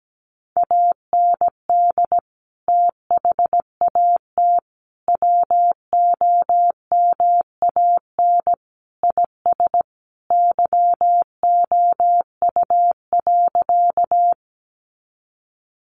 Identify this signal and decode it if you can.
Morse code, heard as a keyed tone, about 17 wpm, 700 Hz. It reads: AND THAT WOMAN IS YOU.
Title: Sentences from Top 1100 Words 17wpm